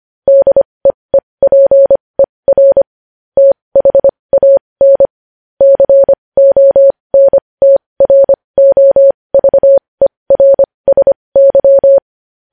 Morse code, heard as a keyed tone, about 25 wpm, spaced as for 20 wpm, 550 Hz. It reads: DEEPER THAN CONTROVERSY
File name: podcast / News headlines-1x-25x20